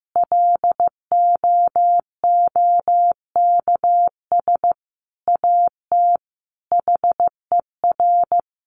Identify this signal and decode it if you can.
Morse code, heard as a keyed tone, about 15 wpm, 700 Hz. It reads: LOOKS AT HER
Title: Additional Easy Phrases 15wpm